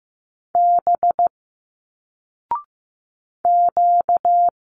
Morse code, heard as a keyed tone, about 15 wpm, 700 Hz. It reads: B Q
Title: Single Letters 15wpm